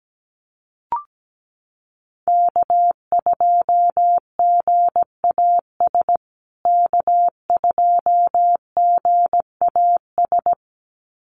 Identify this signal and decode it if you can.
Morse code, heard as a keyed tone, about 17 wpm, 700 Hz. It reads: K2GAS K2GAS
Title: USA Call Signs Repeated 2x 17wpm